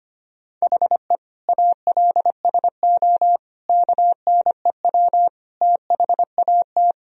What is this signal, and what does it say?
Morse code, heard as a keyed tone, about 25 wpm, 700 Hz. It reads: HE ALSO KNEW THAT
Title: Sentences from Top 1000 Words - Review Part 1 - 25wpm